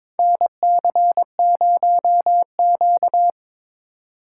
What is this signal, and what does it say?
Morse code, heard as a keyed tone, about 22 wpm, 700 Hz. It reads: NC0Q
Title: USA Call Signs - 3x Character Spacing - 22wpm